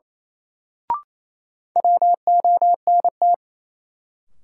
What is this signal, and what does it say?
Morse code, heard as a keyed tone, about 27 wpm, 700 Hz. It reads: WONT
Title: Sentences from Top 1000 Words - Review Part 6 - New Word Drill - 28wpm